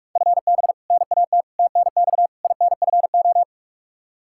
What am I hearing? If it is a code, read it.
Morse code, heard as a keyed tone, about 43 wpm, 700 Hz. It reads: FB NAT TNX INFO
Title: Practice QSO - Exchange 3 - More Info - 45wpm